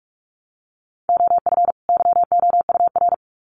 Morse code, heard as a keyed tone, about 44 wpm, 700 Hz. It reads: OF YOUR
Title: Sets of 2 Words 45wpm